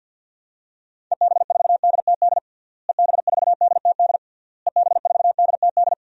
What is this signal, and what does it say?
Morse code, heard as a keyed tone, about 46 wpm, 700 Hz. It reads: EB4DTD EB4DTD EB4DTD
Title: DXCC-100 Call Signs - 6 Characters - Repeated 3x - 50wpm